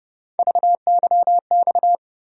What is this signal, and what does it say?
Morse code, heard as a keyed tone, about 30 wpm, 700 Hz. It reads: VYX
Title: USA Call Sign Postfixes - 3 Letters 30wpm